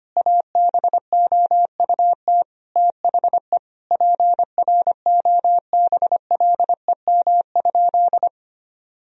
Morse code, heard as a keyed tone, about 25 wpm, 700 Hz. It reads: ABOUT THE PROBLEM?
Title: Sentences from Top 1000 Words - Review Part 4 - 25wpm